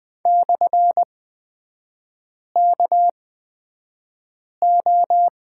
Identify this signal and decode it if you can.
Morse code, heard as a keyed tone, about 20 wpm, spaced as for 5 wpm, 700 Hz. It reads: /KO
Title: 243 Lesson 39e - Letter z - Three Characters 20wpm